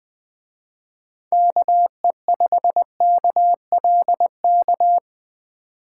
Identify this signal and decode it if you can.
Morse code, heard as a keyed tone, about 20 wpm, 700 Hz. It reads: KE5KLK